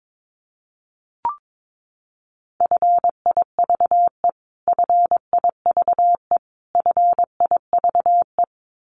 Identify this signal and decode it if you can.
Morse code, heard as a keyed tone, about 22 wpm, 700 Hz. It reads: FIVE FIVE FIVE
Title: Top 500 Words Repeated 22wpm